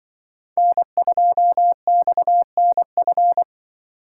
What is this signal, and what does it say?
Morse code, heard as a keyed tone, about 24 wpm, 700 Hz. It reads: N2XNF
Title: Speed Racing - USA 1x3 Call Signs 20wpm